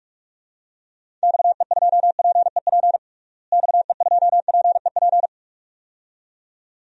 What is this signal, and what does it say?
Morse code, heard as a keyed tone, about 46 wpm, 700 Hz. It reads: XE2PEP XE2PEP
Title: DXCC-100 Call Signs - 6 Characters - Repeated 2x - 45wpm